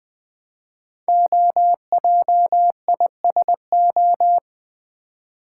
Morse code, heard as a keyed tone, about 20 wpm, 700 Hz. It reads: OJISO
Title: Code Groups - 5 Letters - 20wpm